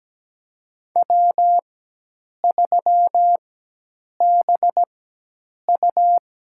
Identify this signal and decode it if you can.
Morse code, heard as a keyed tone, about 17 wpm, spaced as for 8 wpm, 700 Hz. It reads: W3BU
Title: USA Call Signs - 4 Characters - 4x Character Spacing - 17wpm